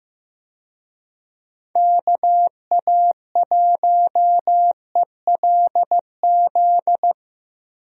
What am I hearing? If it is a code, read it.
Morse code, heard as a keyed tone, about 15 wpm, 700 Hz. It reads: KA1ELZ
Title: USA Call Signs - 6 Characters - 15wpm